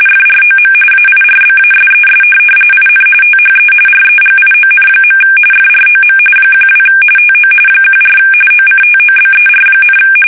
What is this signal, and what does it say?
RTTY, about 45.45 baud. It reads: FROM AIRBASES IN THE REGION. FROM 12 TO 14 OCTOBER,
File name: fsk850 / USS-Massachusetts